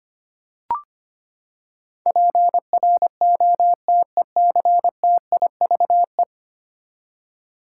Morse code, heard as a keyed tone, about 25 wpm, 700 Hz. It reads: PROTECTIVE
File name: Ten Letter Words - 25wpm